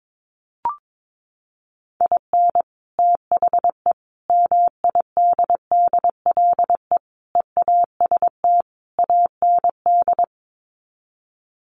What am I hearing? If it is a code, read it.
Morse code, heard as a keyed tone, about 22 wpm, 700 Hz. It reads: IN THE MIDDLE EAST AND
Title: Sets of 5 Words Encore 22wpm